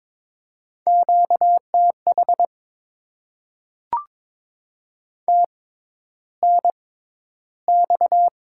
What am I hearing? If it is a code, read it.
Morse code, heard as a keyed tone, about 22 wpm, spaced as for 11 wpm, 700 Hz. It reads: QTH TNX
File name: Top 20 QSO Elements - 6x Character Spacing - 22wpm